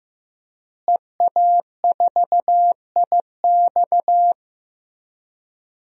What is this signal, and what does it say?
Morse code, heard as a keyed tone, about 15 wpm, 700 Hz. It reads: EA4IX
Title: DXCC-100 Call Signs - 5 Characters - Repeated 3x - 15wpm